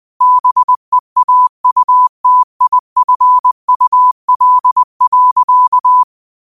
Morse code, heard as a keyed tone, about 20 wpm, 1000 Hz. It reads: BEAUTIFUL.
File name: qotdfiles / 20wpm